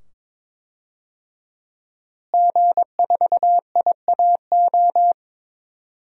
Morse code, heard as a keyed tone, about 22 wpm, 700 Hz. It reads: G4IAO
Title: DXCC-100 Call Signs - 5 Characters - Repeated 2x - 22wpm